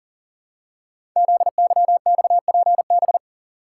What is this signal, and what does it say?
Morse code, heard as a keyed tone, about 39 wpm, 690 Hz. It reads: ZYXPB
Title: Code Groups - 5 Letter:Numbers - 40wpm